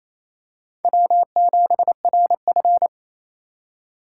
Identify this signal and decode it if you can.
Morse code, heard as a keyed tone, about 27 wpm, 700 Hz. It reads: W7RF
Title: USA Call Signs - 4 Characters - 3x Character Spacing - 28wpm